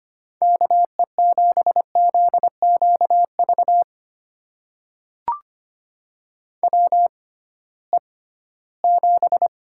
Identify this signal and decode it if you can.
Morse code, heard as a keyed tone, about 25 wpm, spaced as for 16 wpm, 700 Hz. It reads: KE7ZQV WE7
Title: USA Call Signs - 6x Character Spacing - 25wpm